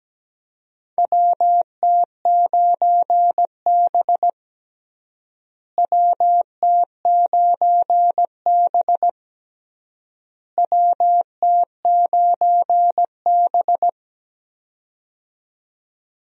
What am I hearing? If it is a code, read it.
Morse code, heard as a keyed tone, about 17 wpm, 700 Hz. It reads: WT9B WT9B WT9B
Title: USA Call Signs - 4 Characters - Repeated 3x - 17wpm